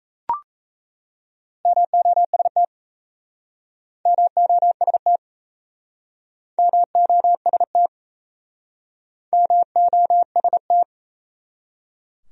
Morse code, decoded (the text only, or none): MOST MOST MOST MOST